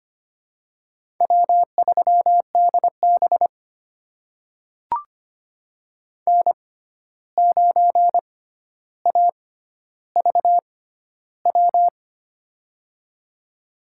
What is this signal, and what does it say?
Morse code, heard as a keyed tone, about 25 wpm, spaced as for 12 wpm, 700 Hz. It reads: W3DB N9AVW
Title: USA Call Signs - 6x Character Spacing - 25wpm